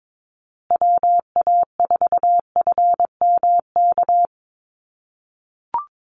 Morse code, heard as a keyed tone, about 22 wpm, 700 Hz. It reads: WA4FMK
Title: USA Call Signs Repeated 3x 22wpm